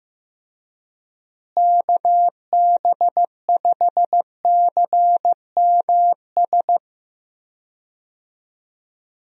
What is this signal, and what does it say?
Morse code, heard as a keyed tone, about 15 wpm, 700 Hz. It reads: KB5CMS
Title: USA Call Signs - 6 Characters - Repeated 3x - 15wpm